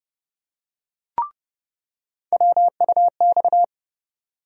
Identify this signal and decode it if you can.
Morse code, heard as a keyed tone, about 30 wpm, 700 Hz. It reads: WUX
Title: USA Call Sign Postfixes - 3 Letters 30wpm